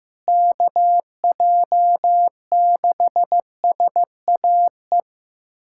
Morse code, heard as a keyed tone, about 16 wpm, 700 Hz. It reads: KJ6SAE
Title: USA Callsign DE Callsign 15wpm